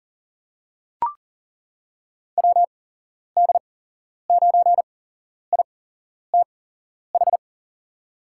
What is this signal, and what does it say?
Morse code, heard as a keyed tone, about 38 wpm, spaced as for 11 wpm, 700 Hz. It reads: WD9ITH
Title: USA Call Signs - 6 Characters - 8x Character Spacing - 40wpm